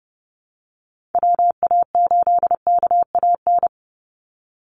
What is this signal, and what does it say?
Morse code, heard as a keyed tone, about 30 wpm, 700 Hz. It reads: WA8KAN
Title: Call Signs Repeated 3x - 30wpm